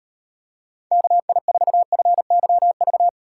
Morse code, heard as a keyed tone, about 38 wpm, 700 Hz. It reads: KI4FYV